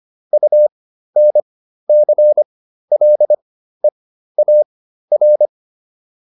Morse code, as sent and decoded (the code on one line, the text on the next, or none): ..- -. -.-. .-.. .- .-.
UNCLEAR